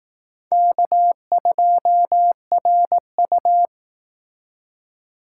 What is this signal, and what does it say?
Morse code, heard as a keyed tone, about 20 wpm, 700 Hz. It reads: K2RU